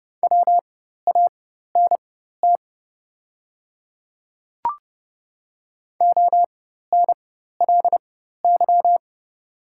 Morse code, heard as a keyed tone, about 30 wpm, spaced as for 14 wpm, 700 Hz. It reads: WANT ONLY
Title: 4x Character Spacing - Top 200 Words - 30wpm